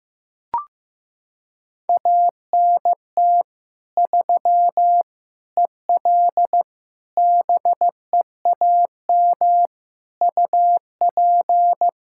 Morse code, heard as a keyed tone, about 15 wpm, 700 Hz. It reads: ANT 3 EL BEAM UP
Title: Practice QSO - Exchange 2 - Info - 15wpm